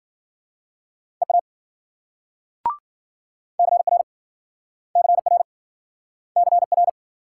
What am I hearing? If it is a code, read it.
Morse code, heard as a keyed tone, about 54 wpm, 700 Hz. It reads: EA CR CR CR